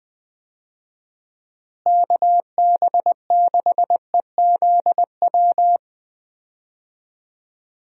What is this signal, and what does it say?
Morse code, heard as a keyed tone, about 20 wpm, 700 Hz. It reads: KB6EZW
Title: USA Call Signs - 6 Characters - Repeated 2x - 20wpm